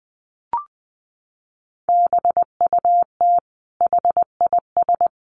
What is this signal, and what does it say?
Morse code, heard as a keyed tone, about 20 wpm, 700 Hz. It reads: BUT HIS
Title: Sentences from Top 1200 Words 20wpm